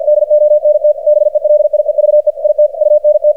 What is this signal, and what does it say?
Morse code, heard as a keyed tone, about 53 wpm, 600 Hz, with radio noise. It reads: DONT BELIEVE IT UNT
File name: archived / CRSnet